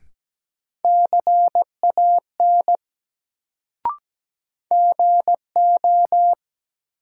Morse code, heard as a keyed tone, about 17 wpm, 700 Hz. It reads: CAN GO
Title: Sentences from Top 100 Words - New Word Drill - 17wpm